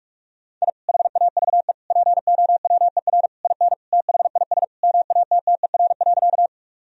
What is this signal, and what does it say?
Morse code, heard as a keyed tone, about 46 wpm, 700 Hz. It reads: I HAVE POWER IN THIS MATTER.
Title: Sentences from Top 1000 Words - Review Part 2 - 45wpm